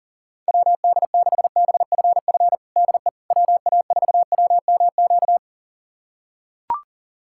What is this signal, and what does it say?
Morse code, heard as a keyed tone, about 40 wpm, 700 Hz. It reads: WD6BFF DE WA4WMQ
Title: USA Callsign DE Callsign - 6 Characters - 40wpm